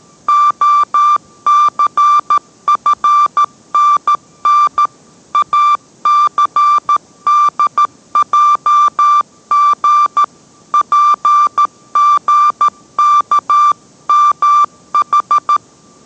Morse code, heard as a keyed tone, about 13 wpm, 1180 Hz. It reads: OCFNN ACDJG PGKMH